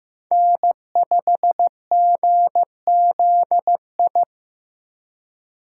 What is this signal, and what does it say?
Morse code, heard as a keyed tone, about 15 wpm, 700 Hz. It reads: N5GZI
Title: USA Call Signs - 5 Characters - 2x Character Spacing - 15wpm